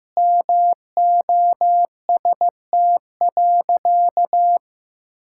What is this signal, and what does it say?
Morse code, heard as a keyed tone, about 15 wpm, 700 Hz. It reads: MOST.
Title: Sentences from Top 1600 Words 15wpm